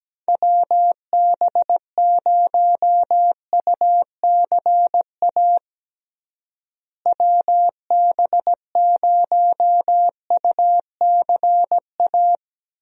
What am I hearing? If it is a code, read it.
Morse code, heard as a keyed tone, about 17 wpm, 700 Hz. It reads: WB0UCA WB0UCA
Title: USA Call Signs - 6 Characters - Repeated 2x - 17wpm